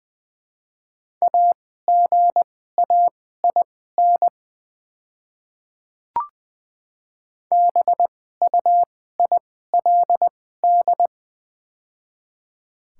Morse code, heard as a keyed tone, about 20 wpm, spaced as for 14 wpm, 700 Hz. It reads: AGAIN BUILD
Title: 2x Character Spacing - Top 200 Words - 20wpm